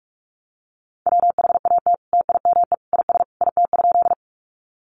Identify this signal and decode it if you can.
Morse code, heard as a keyed tone, about 44 wpm, 700 Hz. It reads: WHAT TIME IS IT?
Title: Sentences from Top 100 Words 45wpm